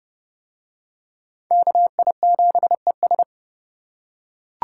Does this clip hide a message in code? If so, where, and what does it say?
Yes: Morse code from 1.5 to 3.2 s, heard as a keyed tone, about 30 wpm, 700 Hz. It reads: KI7ES